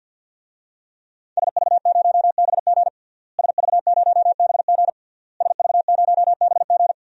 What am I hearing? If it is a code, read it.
Morse code, heard as a keyed tone, about 51 wpm, 700 Hz. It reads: SV0BG SV0BG SV0BG
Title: DXCC-100 Call Signs - 5 Characters - Repeated 3x - 50wpm